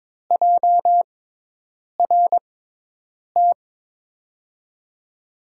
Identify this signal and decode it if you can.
Morse code, heard as a keyed tone, about 21 wpm, spaced as for 8 wpm, 700 Hz. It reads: JRT
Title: USA Call Signs - 5 Characters - 6x Character Spacing - 22wpm